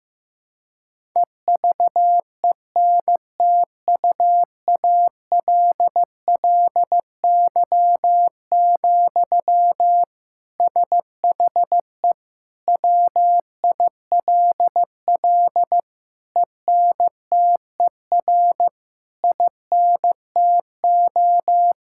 Morse code, heard as a keyed tone, about 15 wpm, 700 Hz. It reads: EVENTUALLY, SHE WILL ENTER INTO
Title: Sentences from Top 1000 Words - Review Part 6 - 15wpm